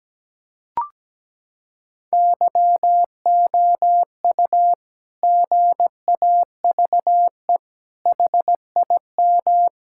Morse code, heard as a keyed tone, about 17 wpm, 700 Hz. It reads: YOU GAVE HIM
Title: Sentences from Top 1000 Words - Review Part 6 - 17wpm